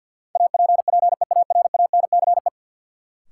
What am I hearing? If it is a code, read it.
Morse code, heard as a keyed tone, about 52 wpm, 700 Hz. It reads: APPEARANCE
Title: Sentences from Top 1100 Words - New Word Drill Alt - 50wpm